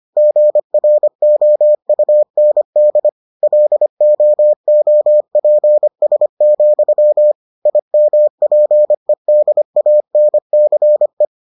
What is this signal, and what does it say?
Morse code, heard as a keyed tone, about 25 wpm, 600 Hz. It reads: GROUND LOOPS, IMPEDANCE